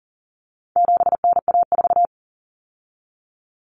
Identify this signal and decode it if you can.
Morse code, heard as a keyed tone, about 40 wpm, 700 Hz. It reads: 7NA4
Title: Common Cut Numbers - 4-Digit 40wpm